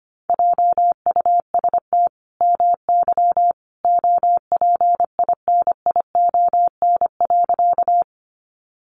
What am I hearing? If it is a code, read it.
Morse code, heard as a keyed tone, about 25 wpm, 700 Hz. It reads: JUST MY OPINION.
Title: Sentences from Top 1200 Words 25wpm